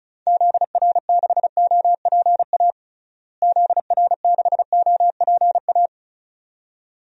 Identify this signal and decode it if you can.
Morse code, heard as a keyed tone, about 36 wpm, 700 Hz. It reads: ZR6OPA ZR6OPA
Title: DXCC-100 Call Signs - 6 Characters - Repeated 2x - 35wpm